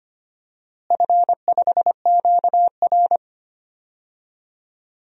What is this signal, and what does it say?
Morse code, heard as a keyed tone, about 25 wpm, 700 Hz. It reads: F5QR
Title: DXCC-100 Call Signs - 4 Characters - Repeated 2x - 25wpm